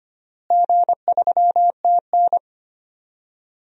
Morse code, heard as a keyed tone, about 25 wpm, 700 Hz. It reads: G3TN
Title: DXCC-100 Call Signs - 4 Characters - Repeated 3x - 25wpm